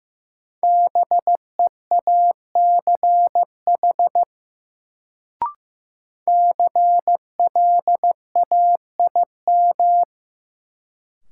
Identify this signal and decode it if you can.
Morse code, heard as a keyed tone, about 16 wpm, 700 Hz. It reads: BEACH CLAIM
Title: Sentences from Top 1500 Words - New Word Drill Alt - 15wpm